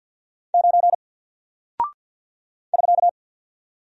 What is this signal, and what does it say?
Morse code, heard as a keyed tone, about 48 wpm, 700 Hz. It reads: 9 <SK>
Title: Single Character (Letter:Number:Punctuation:Prosign) Alt 50wpm